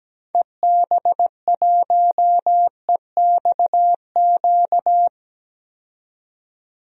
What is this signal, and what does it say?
Morse code, heard as a keyed tone, about 17 wpm, 700 Hz. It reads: EB1EXQ
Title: DXCC-100 Call Signs - 6 Characters - 17wpm